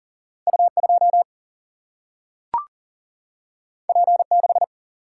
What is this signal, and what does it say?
Morse code, heard as a keyed tone, about 40 wpm, 700 Hz. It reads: U2 P6
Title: Single-Letter Single-Number 40wpm